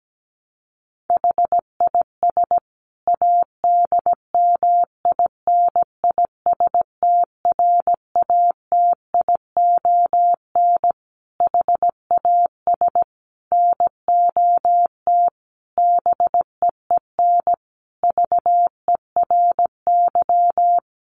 Morse code, heard as a keyed tone, about 17 wpm, 700 Hz. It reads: HIS ADMINISTRATION HAS NOT BEEN VERY
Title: Sentences from Top 1300 Words 17wpm